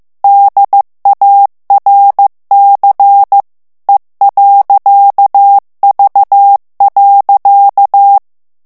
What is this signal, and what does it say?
Morse code, heard as a keyed tone, about 15 wpm, 800 Hz. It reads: DARC E.V.
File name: ALLEMAND texte clair 1 vitesse 14 mots minute